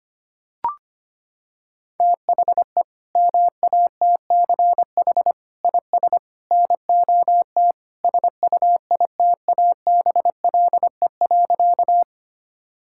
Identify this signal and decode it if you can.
Morse code, heard as a keyed tone, about 25 wpm, 700 Hz. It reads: THE MATCH IS NOT SUITABLE.
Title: Sentences from Top 1600 Words 25wpm